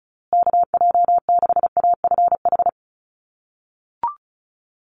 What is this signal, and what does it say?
Morse code, heard as a keyed tone, about 34 wpm, 700 Hz. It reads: KJ6AFH
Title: USA Call Signs Repeated 3x 35wpm